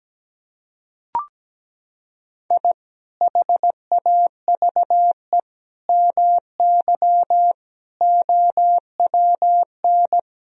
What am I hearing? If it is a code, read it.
Morse code, heard as a keyed tone, about 17 wpm, 700 Hz. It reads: I HAVE MY OWN
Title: Sentences from Top 1000 Words - Review Part 2 - 17wpm